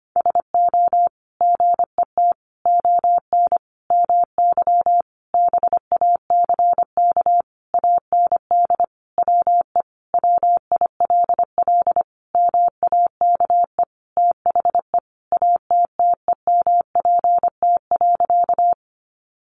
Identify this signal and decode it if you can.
Morse code, heard as a keyed tone, about 25 wpm, 700 Hz. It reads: SO GET ON MY BACK AND WE WILL MAKE THE ATTEMPT.
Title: Sentences from Top 1000 Words 25wpm